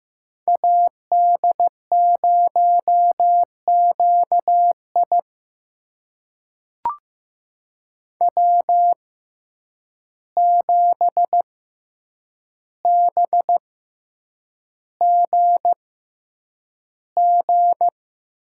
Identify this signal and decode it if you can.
Morse code, heard as a keyed tone, about 15 wpm, spaced as for 8 wpm, 700 Hz. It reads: AD0QI W7BGG
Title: USA Call Signs - 5 Characters - 6x Character Spacing - 15wpm